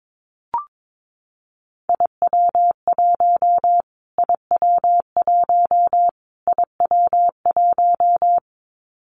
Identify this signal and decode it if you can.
Morse code, heard as a keyed tone, about 22 wpm, 700 Hz. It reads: IW1 IW1 IW1
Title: Callsign Prefixes for Top 25 DXCC - Repeated 3x - 22wpm